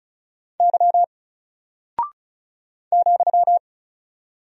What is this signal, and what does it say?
Morse code, heard as a keyed tone, about 36 wpm, 700 Hz. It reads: Y ,
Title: Single Character (Letter:Number:Punctuation:Prosign) Alt 35wpm